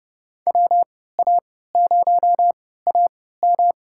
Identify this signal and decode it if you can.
Morse code, heard as a keyed tone, about 31 wpm, spaced as for 17 wpm, 700 Hz. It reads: WA0AM
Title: USA Call Signs - 5 Characters - 3x Character Spacing - 30wpm